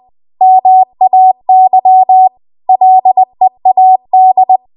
Morse code, heard as a keyed tone, about 20 wpm, 750 Hz. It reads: MAY LEAD
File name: Archive / 20 WPM / 20WPM